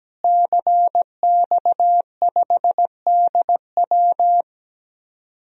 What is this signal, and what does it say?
Morse code, heard as a keyed tone, about 17 wpm, 700 Hz. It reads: CX5DW
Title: DXCC-100 Call Signs - 5 Characters - Repeated 3x - 17wpm